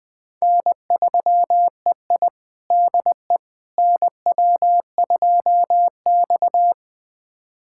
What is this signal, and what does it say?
Morse code, heard as a keyed tone, about 21 wpm, 700 Hz. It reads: N3EI DE NW2X
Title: USA Callsign DE Callsign - 4 Characters - 20wpm